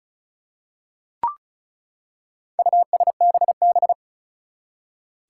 Morse code, heard as a keyed tone, about 34 wpm, 700 Hz. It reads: USBB
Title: Four Challenging Letter-Number Combinations 35wpm